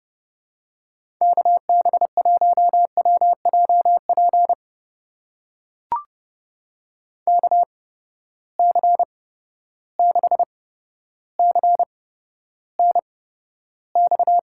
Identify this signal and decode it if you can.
Morse code, heard as a keyed tone, about 30 wpm, spaced as for 13 wpm, 700 Hz. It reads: KB1WJP KC6CNX
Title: USA Call Signs - 6 Characters - 8x Character Spacing - 30wpm